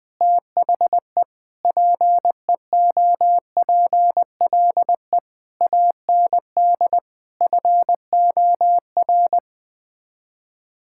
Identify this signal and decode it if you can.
Morse code, heard as a keyed tone, about 20 wpm, 700 Hz. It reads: THE PEOPLE AND FOR
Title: Sets of 8 Words - 20wpm